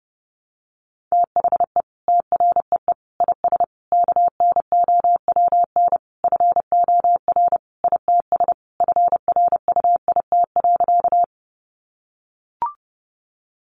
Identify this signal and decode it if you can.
Morse code, heard as a keyed tone, about 30 wpm, 700 Hz. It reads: THE TREE IS KNOWN FOR ITS FRUIT.
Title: Sentences from Top 1300 Words 30wpm